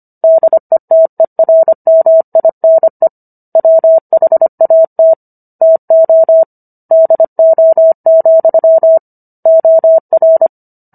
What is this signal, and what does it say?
Morse code, heard as a keyed tone, about 25 wpm, 650 Hz. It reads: DETERMINE WHAT TO DO, OR